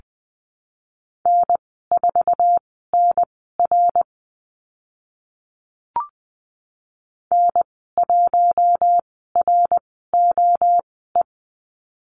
Morse code, heard as a keyed tone, about 20 wpm, spaced as for 14 wpm, 700 Hz. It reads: N4NR N1ROE
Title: 062 Lesson 11g - Letter h - Call Signs 20wpm